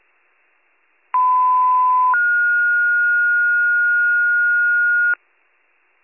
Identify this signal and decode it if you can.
Two-tone page, heard as a one-second tone, then a three-second tone, about 1010 then 1430 Hz, with noise. It is heard from 1.1 to 5.1 s.